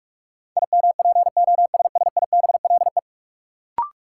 Morse code, heard as a keyed tone, about 46 wpm, 700 Hz. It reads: IMPOSSIBLE
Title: Sentences from Top 1100 Words - New Word Drill Alt - 45wpm